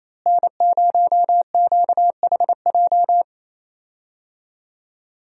Morse code, heard as a keyed tone, about 28 wpm, 700 Hz. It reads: N0QHJ
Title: USA Call Signs - 5 Characters - Repeated 2x - 28wpm